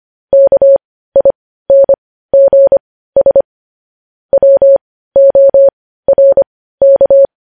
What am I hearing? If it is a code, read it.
Morse code, heard as a keyed tone, about 25 wpm, spaced as for 15 wpm, 550 Hz. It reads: KINGS WORK
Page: News Headlines in Morse Code at 15 WPM